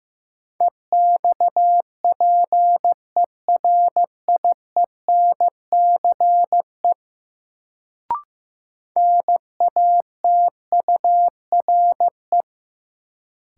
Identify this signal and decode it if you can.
Morse code, heard as a keyed tone, about 15 wpm, 700 Hz. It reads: EXPERIENCE NATURE
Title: Sentences from Top 600 Words - New Word Drill - 15wpm